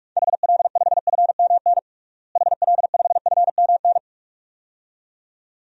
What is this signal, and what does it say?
Morse code, heard as a keyed tone, about 46 wpm, 700 Hz. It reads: HL5FMN HL5FMN
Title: DXCC-100 Call Signs - 6 Characters - Repeated 2x - 45wpm